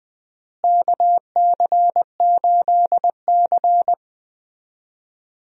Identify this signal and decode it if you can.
Morse code, heard as a keyed tone, about 20 wpm, 700 Hz. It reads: KC8C
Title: USA Call Signs - 4 Characters - Repeated 3x - 20wpm